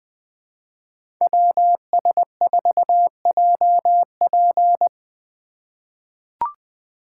Morse code, heard as a keyed tone, about 20 wpm, 700 Hz. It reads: WS4JP
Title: USA Call Signs - 5 Characters - 2x Character Spacing - 20wpm